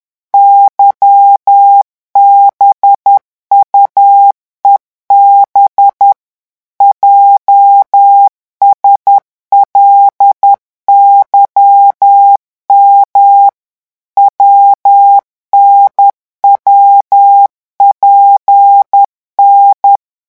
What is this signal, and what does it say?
Morse code, heard as a keyed tone, about 10 wpm, 800 Hz. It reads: YBUEB JSLYM WNWPN
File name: morse mp3 / revision8-vitesse 10 mots